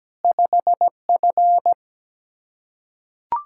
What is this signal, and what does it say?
Morse code, heard as a keyed tone, about 17 wpm, 700 Hz. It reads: 5F
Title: Practice QSO - Exchange 2 - Info - 17wpm